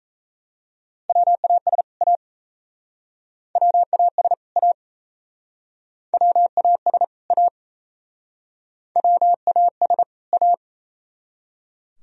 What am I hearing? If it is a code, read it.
Morse code, heard as a keyed tone, about 38 wpm, 700 Hz. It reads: WAS A WAS A WAS A WAS A